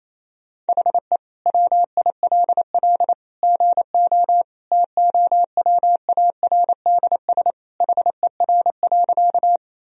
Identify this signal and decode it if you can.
Morse code, heard as a keyed tone, about 28 wpm, 700 Hz. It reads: HE WILL GO TOWARDS HER.
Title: Sentences from Top 1000 Words - Review Part 2 - 28wpm